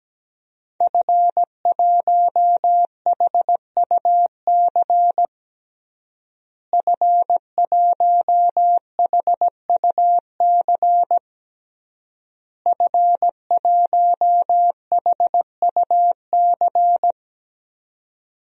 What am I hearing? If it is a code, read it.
Morse code, heard as a keyed tone, about 17 wpm, 700 Hz. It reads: F1HUC F1HUC F1HUC
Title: DXCC-100 Call Signs - 5 Characters - Repeated 3x - 17wpm